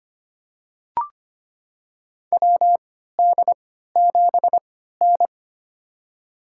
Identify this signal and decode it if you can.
Morse code, heard as a keyed tone, about 25 wpm, spaced as for 14 wpm, 700 Hz. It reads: WD7N
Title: USA Call Signs - 4 Characters - 3x Character Spacing - 25wpm